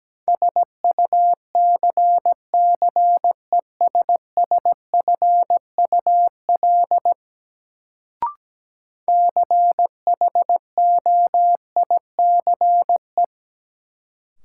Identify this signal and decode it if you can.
Morse code, heard as a keyed tone, about 17 wpm, 700 Hz. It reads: SUCCESSFUL CHOICE
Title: Sentences from Top 900 Words - New Word Drill - 17wpm